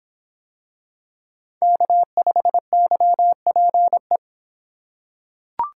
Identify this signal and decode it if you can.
Morse code, heard as a keyed tone, about 26 wpm, 700 Hz. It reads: K5YPE